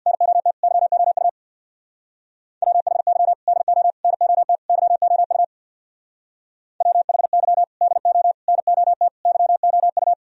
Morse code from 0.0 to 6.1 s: NOT YOU WHY DO NOT YOU

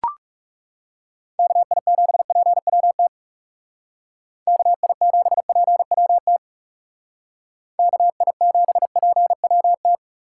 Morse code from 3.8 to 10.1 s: KI7PWT KI7PWT

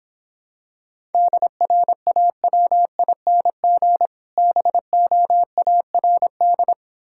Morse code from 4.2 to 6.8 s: BOARD

